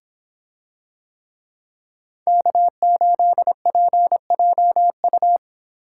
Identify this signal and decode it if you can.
Morse code, heard as a keyed tone, about 25 wpm, 700 Hz. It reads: K8PJU